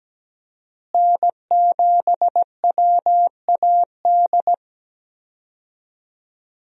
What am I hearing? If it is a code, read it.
Morse code, heard as a keyed tone, about 17 wpm, 700 Hz. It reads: N7WAD
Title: USA Call Signs - 5 Characters - Repeated 2x - 17wpm